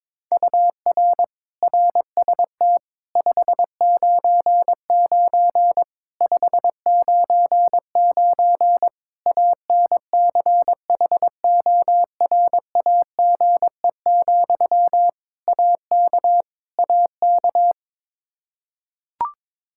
Morse code, heard as a keyed tone, about 22 wpm, 700 Hz. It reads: UR RST 599 599 ANCHORAGE, AK AK
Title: Practice QSO - Exchange 1 - Beginning - 22wpm